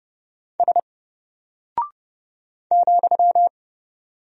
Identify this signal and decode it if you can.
Morse code, heard as a keyed tone, about 30 wpm, 700 Hz. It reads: S ,
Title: Single Character (Letter:Number:Punctuation:Prosign) Alt 30wpm